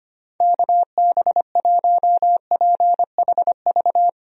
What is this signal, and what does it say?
Morse code, heard as a keyed tone, about 25 wpm, 700 Hz. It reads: KB1PHV
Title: USA Call Signs - 6 Characters - Repeated 2x - 25wpm